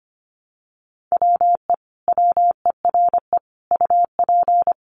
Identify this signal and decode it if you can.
Morse code, heard as a keyed tone, about 25 wpm, 700 Hz. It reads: WE WERE UP
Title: Sentences from Top 400 Words 25wpm